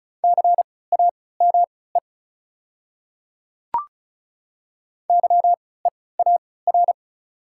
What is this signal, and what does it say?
Morse code, heard as a keyed tone, about 36 wpm, spaced as for 20 wpm, 700 Hz. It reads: CAME YEAR
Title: 3x Character Spacing - Top 500 Words - 35wpm